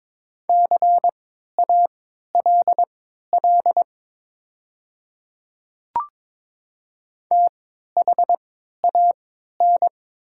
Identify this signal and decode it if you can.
Morse code, heard as a keyed tone, about 22 wpm, spaced as for 12 wpm, 700 Hz. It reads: CALL THAN
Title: 3x Character Spacing - Top 100 Words - 22wpm